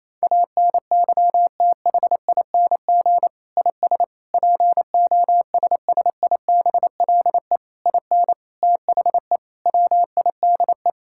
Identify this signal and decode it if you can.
Morse code, heard as a keyed tone, about 28 wpm, 700 Hz. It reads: ANYTHING IS POSSIBLE IN THE WIDE
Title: Sentences from Top 1000 Words - Review Part 6 - 28wpm